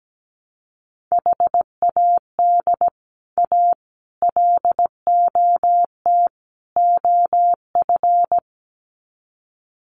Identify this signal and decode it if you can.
Morse code, heard as a keyed tone, about 17 wpm, 700 Hz. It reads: HAD A LOT OF
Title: Sets of 4 Words 17wpm